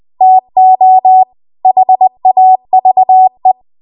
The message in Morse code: TO HAVE